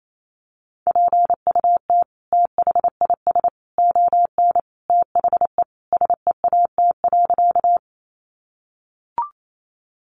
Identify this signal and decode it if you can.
Morse code, heard as a keyed tone, about 28 wpm, 700 Hz. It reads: PUT THIS ON THE SEAT.
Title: Sentences from Top 1500 Words 28wpm